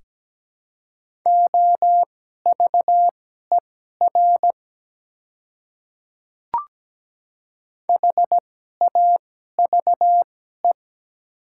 Morse code, heard as a keyed tone, about 17 wpm, spaced as for 12 wpm, 690 Hz. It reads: OVER HAVE
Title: 2x Character Spacing - Top 100 Words - 17wpm